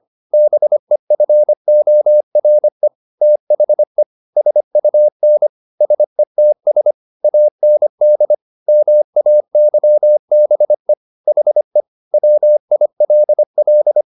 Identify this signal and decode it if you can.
Morse code, heard as a keyed tone, about 25 wpm, 600 Hz. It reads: BEFORE THE SUN SETS AND MAYBE HE WILL
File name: the old man and the sea part 1 25wpm